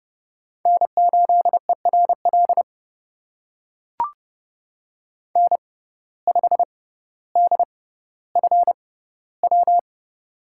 Morse code, heard as a keyed tone, about 30 wpm, spaced as for 15 wpm, 700 Hz. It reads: N8ERL N5DFW
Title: USA Call Signs - 5 Characters - 6x Character Spacing - 30wpm